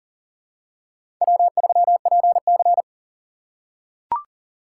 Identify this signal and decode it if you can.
Morse code, heard as a keyed tone, about 40 wpm, 700 Hz. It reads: W3PC
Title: USA Call Signs - 4 Characters - 3x Character Spacing - 40wpm